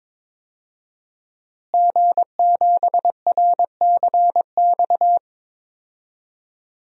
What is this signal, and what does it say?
Morse code, heard as a keyed tone, about 21 wpm, 700 Hz. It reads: G7RCX
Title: DXCC-100 Call Signs - 5 Characters - Repeated 3x - 22wpm